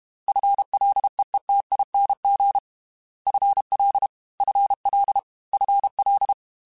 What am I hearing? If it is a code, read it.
Morse code, heard as a keyed tone, about 32 wpm, 800 Hz. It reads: FLEETING FL FL FL
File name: FL / FL mp3s / FL Prefixes 3x2 Twice @30